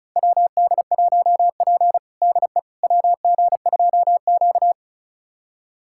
Morse code, heard as a keyed tone, about 36 wpm, 700 Hz. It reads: WD1P DE WG2Q
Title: USA Callsign DE Callsign - 4 Characters - 35wpm